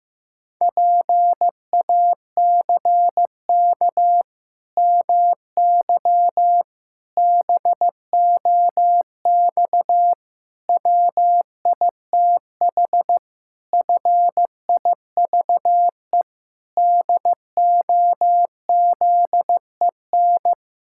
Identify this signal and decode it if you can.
Morse code, heard as a keyed tone, about 15 wpm, 700 Hz. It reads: PACK MY BOX WITH FIVE DOZEN
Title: Pangram Challenge v2 - 15wpm